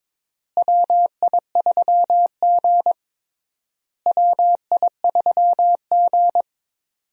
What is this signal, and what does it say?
Morse code, heard as a keyed tone, about 22 wpm, 700 Hz. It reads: WI3G WI3G
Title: USA Call Signs - 4 Characters - Repeated 3x - 22wpm